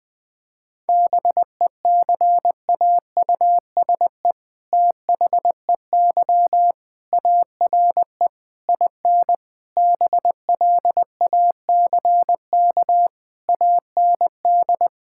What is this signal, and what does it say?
Morse code, heard as a keyed tone, about 20 wpm, 700 Hz. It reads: BECAUSE THEY ARE IN BLACK AND
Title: All-Age Appropriate Jokes - 20wpm